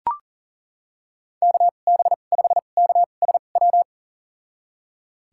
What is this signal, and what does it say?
Morse code, heard as a keyed tone, about 40 wpm, spaced as for 29 wpm, 700 Hz. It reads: KB5KSW
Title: USA Call Signs - 2x Character Spacing - 40wpm